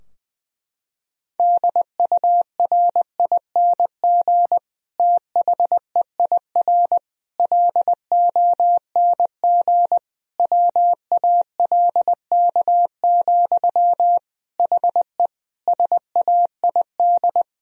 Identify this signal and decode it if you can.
Morse code, heard as a keyed tone, about 20 wpm, 700 Hz. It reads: DURING THEIR LONG WALK, HE SAID
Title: Sentences from Top 1000 Words - Review Part 2 - 20wpm